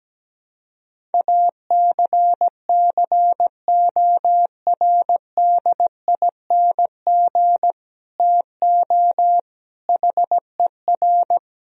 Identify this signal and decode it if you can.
Morse code, heard as a keyed tone, about 17 wpm, 700 Hz. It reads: ACCORDING TO HER
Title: Sentences from Top 1000 Words - Review Part 4 - 17wpm